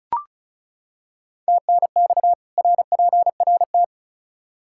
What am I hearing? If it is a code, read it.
Morse code, heard as a keyed tone, about 36 wpm, 700 Hz. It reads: TNX RPRT
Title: Practice QSO - Exchange 2 - Info - 35wpm